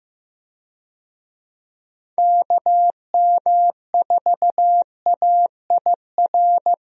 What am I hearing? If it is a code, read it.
Morse code, heard as a keyed tone, about 15 wpm, 700 Hz. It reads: KM4AIR
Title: USA Call Signs - 6 Characters - Repeated 2x - 15wpm